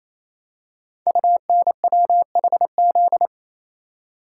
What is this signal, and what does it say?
Morse code, heard as a keyed tone, about 27 wpm, 700 Hz. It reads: UNWHZ
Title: Code Groups - 5 Letters - 28wpm